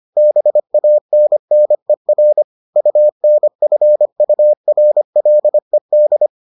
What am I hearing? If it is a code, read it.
Morse code, heard as a keyed tone, about 25 wpm, 600 Hz. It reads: BANNER UNFURLED